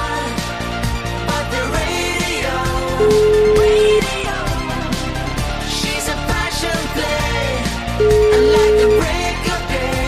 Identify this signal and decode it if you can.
Ringback tone (Europe, 425 Hz), with noise.